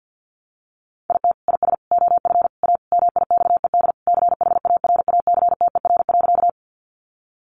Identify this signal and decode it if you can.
Morse code, heard as a keyed tone, about 50 wpm, 700 Hz. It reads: IT IS OF A MIXED CHARACTER.
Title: Sentences from Top 1500 Words 50wpm